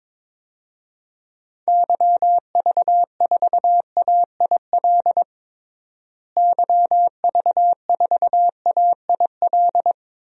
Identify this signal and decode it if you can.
Morse code, heard as a keyed tone, about 22 wpm, 700 Hz. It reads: YV4AIL YV4AIL
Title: DXCC-100 Call Signs - 6 Characters - Repeated 2x - 22wpm